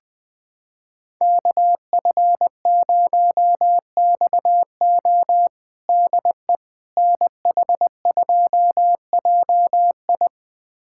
Morse code, heard as a keyed tone, about 20 wpm, 700 Hz. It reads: KF0XO DE NH2JI
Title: USA Callsign DE Callsign - 5 Characters - 20wpm